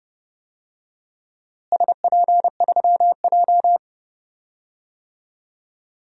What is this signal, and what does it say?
Morse code, heard as a keyed tone, about 30 wpm, 700 Hz. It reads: SP3J
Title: DXCC-100 Call Signs - 4 Characters - Repeated 3x - 30wpm